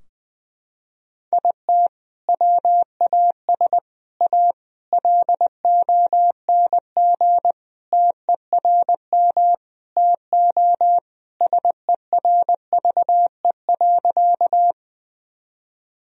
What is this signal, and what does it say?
Morse code, heard as a keyed tone, about 20 wpm, 700 Hz. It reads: IT WAS A LONG TERM TO SERVE.
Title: Sentences from Top 1000 Words - Review Part 4 - 20wpm